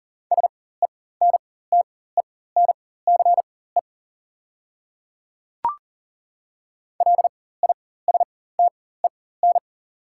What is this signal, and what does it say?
Morse code, heard as a keyed tone, about 38 wpm, spaced as for 18 wpm, 690 Hz. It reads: SENTENCE LISTEN
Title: 4x Character Spacing - Top 500 Words - 40wpm